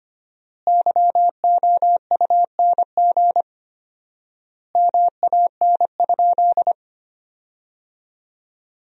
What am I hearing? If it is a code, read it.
Morse code, heard as a keyed tone, about 25 wpm, 700 Hz. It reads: YOUNG MAN?
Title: 4x Word-Spacing - Sentences from Top 400 Words - 25wpm